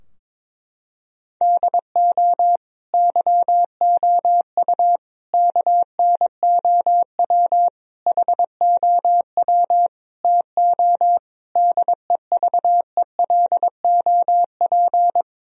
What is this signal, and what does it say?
Morse code, heard as a keyed tone, about 22 wpm, 700 Hz. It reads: DO YOU KNOW HOW TO DEVELOP
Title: Sentences from Top 1000 Words - Review Part 3 - 22wpm